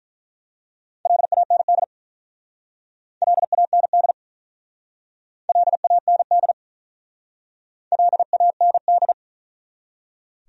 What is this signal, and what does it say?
Morse code, heard as a keyed tone, about 44 wpm, 700 Hz. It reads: LAND LAND LAND LAND